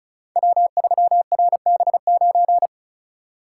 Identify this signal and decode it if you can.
Morse code, heard as a keyed tone, about 36 wpm, 700 Hz. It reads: W3RB9
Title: Code Groups - 5 Letter:Numbers - 35wpm